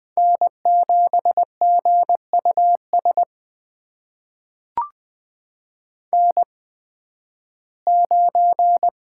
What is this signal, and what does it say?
Morse code, heard as a keyed tone, about 20 wpm, spaced as for 13 wpm, 700 Hz. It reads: N7GUS N9
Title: USA Call Signs - 5 Characters - 8x Character Spacing - 20wpm